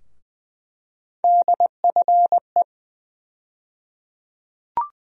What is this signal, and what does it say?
Morse code, heard as a keyed tone, about 20 wpm, 700 Hz. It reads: DFE
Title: USA Call Sign Postfixes - 3 Letters 20wpm